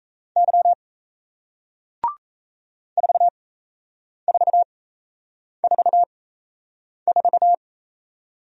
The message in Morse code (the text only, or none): Y 4444